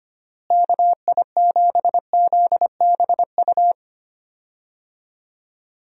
Morse code, heard as a keyed tone, about 25 wpm, 700 Hz. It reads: KI7ZBU
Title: USA Call Signs - 6 Characters - Repeated 3x - 25wpm